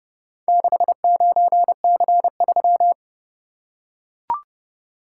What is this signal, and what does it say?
Morse code, heard as a keyed tone, about 27 wpm, 700 Hz. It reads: 69C3